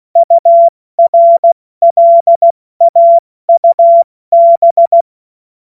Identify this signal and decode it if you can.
Morse code, heard as a keyed tone, about 16 wpm, 680 Hz. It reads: URLAUB
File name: Witz-16-September-2025-70-F80